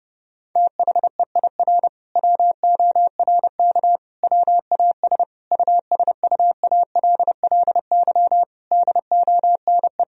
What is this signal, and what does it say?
Morse code, heard as a keyed tone, about 30 wpm, 700 Hz. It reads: THEIR WORK WAS USUALLY DONE
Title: Sentences from Top 1000 Words - Review Part 3 - 30wpm